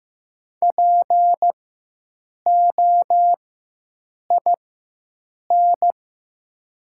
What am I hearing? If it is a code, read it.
Morse code, heard as a keyed tone, about 15 wpm, spaced as for 7 wpm, 700 Hz. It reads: POIN
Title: 4x Character Spacing - Top 200 Words - 15wpm